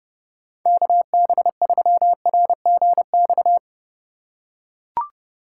Morse code, heard as a keyed tone, about 30 wpm, 700 Hz. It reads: KB3RGX
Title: USA Call Signs - 6 Characters - 4x Character Spacing - 30wpm